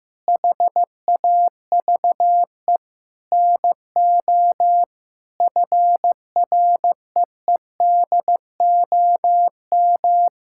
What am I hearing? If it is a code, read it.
Morse code, heard as a keyed tone, about 15 wpm, 700 Hz. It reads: HAVE NO FREEDOM
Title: Sentences from Top 1000 Words - Review Part 5 - 15wpm